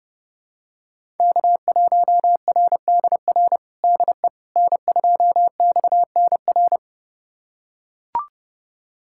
Morse code, heard as a keyed tone, about 30 wpm, 700 Hz. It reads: K1RDR DE N2XNR
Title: USA Callsign DE Callsign - 5 Characters - 30wpm